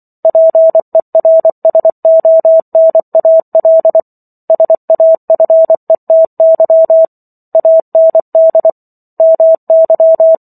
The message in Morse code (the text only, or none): PERSONAL SAFETY AND MY